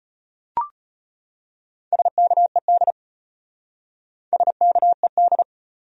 Morse code, heard as a keyed tone, about 33 wpm, 700 Hz. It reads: SKED SKED